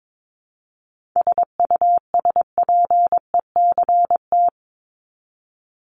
Morse code, heard as a keyed tone, about 22 wpm, 700 Hz. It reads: SUSPECT
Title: Top 1751-2000 Words 22wpm